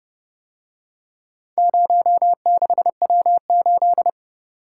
Morse code, heard as a keyed tone, about 29 wpm, 700 Hz. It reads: 06W8